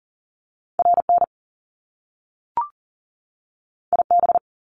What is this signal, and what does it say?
Morse code, heard as a keyed tone, about 42 wpm, 700 Hz. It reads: RN IB
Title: Top Two Letter Combinations in Words 40wpm